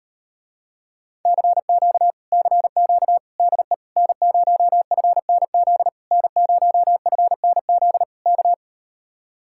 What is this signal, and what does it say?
Morse code, heard as a keyed tone, about 38 wpm, 700 Hz. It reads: CQ CQ DE N0FNZ N0FNZ K